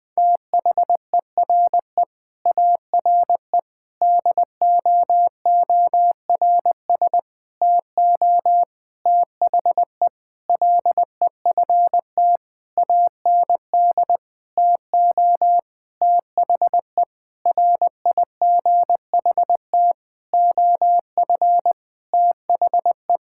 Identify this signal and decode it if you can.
Morse code, heard as a keyed tone, about 20 wpm, 700 Hz. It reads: THERE ARE DOORS TO THE LEFT AND TO THE RIGHT OF THE
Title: Sentences with 8-Word Phrases - 20wpm